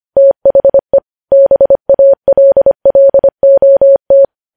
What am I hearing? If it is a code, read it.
Morse code, heard as a keyed tone, about 25 wpm, 550 Hz. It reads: THE BALLOT
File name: podcast / News headlines-1x-25x25